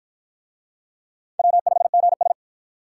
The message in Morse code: W5GS

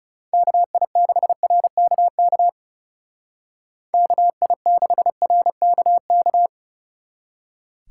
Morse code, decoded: KI6RKK KI6RKK

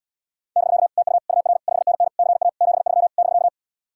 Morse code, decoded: <BK> TU UR 5NN CA 73 <BK>

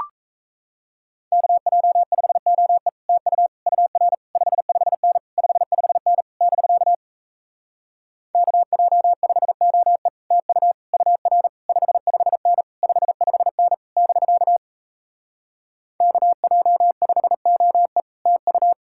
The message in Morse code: KJ5OE TU UR 55N 55N <BK> KJ5OE TU UR 55N 55N <BK> KJ5OE TU

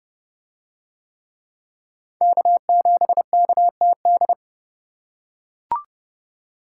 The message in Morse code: K7KTD

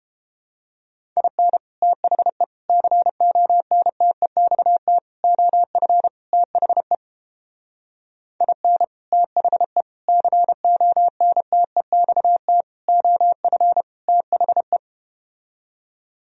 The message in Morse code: IN THE CONTEXT OF THE IN THE CONTEXT OF THE

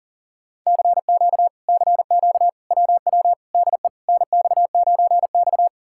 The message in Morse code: CQ CQ WW DE NX9X

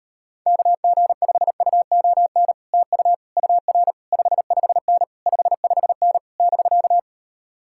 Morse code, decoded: KG5UON TU UR 55N 55N <BK>

